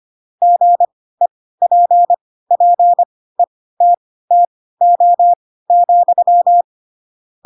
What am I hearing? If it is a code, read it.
Morse code, heard as a keyed tone, about 25 wpm, spaced as for 16 wpm, 700 Hz. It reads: GEPPETTO,